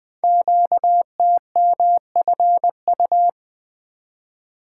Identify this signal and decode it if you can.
Morse code, heard as a keyed tone, about 20 wpm, 700 Hz. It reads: QTMFU
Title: Code Groups - 5 Letters - 20wpm